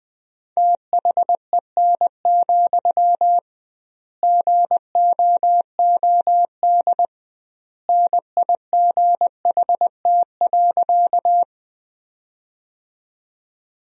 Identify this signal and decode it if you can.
Morse code, heard as a keyed tone, about 20 wpm, 700 Hz. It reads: THEN, GOOD NIGHT.
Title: 2x Word-Spacing - Sentences from Top 300 Words - 20wpm